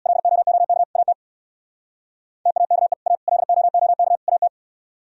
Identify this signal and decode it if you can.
Morse code, heard as a keyed tone, about 62 wpm, 700 Hz. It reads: LOOK AT TAKE A LOOK AT
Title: Speed-Racing - Sets of 4 Words 50wpm